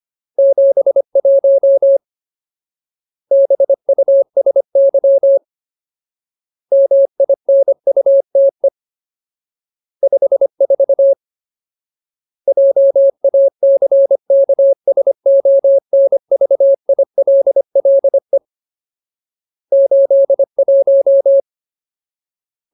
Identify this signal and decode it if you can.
Morse code, heard as a keyed tone, about 25 wpm, 550 Hz. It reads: 71 BUSY MINUTE 54 JACKSONVILLE 81
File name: Ultimate Word List / 25WPM Ultimate